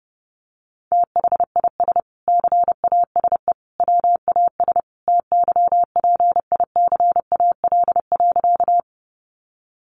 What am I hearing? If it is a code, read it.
Morse code, heard as a keyed tone, about 30 wpm, 700 Hz. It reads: THIS CASE WAS TYPICAL.
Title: Sentences from Top 1100 Words 30wpm